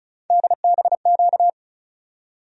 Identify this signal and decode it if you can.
Morse code, heard as a keyed tone, about 34 wpm, 700 Hz. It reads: DBQ
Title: USA Call Sign Postfixes - 3 Letters 35wpm